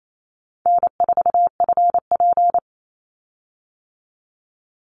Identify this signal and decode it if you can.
Morse code, heard as a keyed tone, about 28 wpm, 700 Hz. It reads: N4FP
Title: Easy Call Signs Repeated 2x - 28wpm